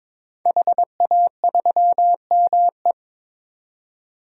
Morse code, heard as a keyed tone, about 22 wpm, 700 Hz. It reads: HA3ME
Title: DXCC-100 Call Signs - 5 Characters - 22wpm